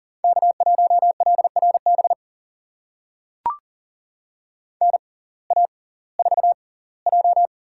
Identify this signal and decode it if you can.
Morse code, heard as a keyed tone, about 38 wpm, spaced as for 22 wpm, 700 Hz. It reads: K1LRB NA4J
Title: USA Call Signs - 6x Character Spacing - 40wpm